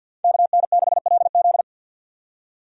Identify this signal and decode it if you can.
Morse code, heard as a keyed tone, about 46 wpm, 700 Hz. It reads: KN6LZ
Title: USA Call Signs - 5 Characters - 50wpm